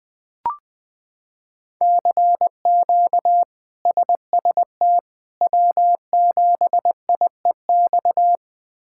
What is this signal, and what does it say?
Morse code, heard as a keyed tone, about 20 wpm, 700 Hz. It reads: CQ SST W7IEX
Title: K1USN SST QSO Practice - 20wpm